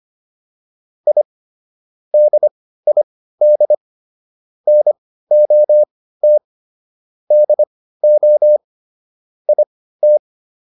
Morse code, heard as a keyed tone, about 25 wpm, spaced as for 15 wpm, 600 Hz. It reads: I DID NOT DO IT
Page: intermediate-practice-files